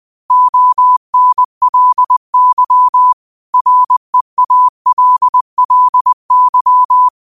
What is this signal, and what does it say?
Morse code, heard as a keyed tone, about 20 wpm, 1000 Hz. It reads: ONLY REALLY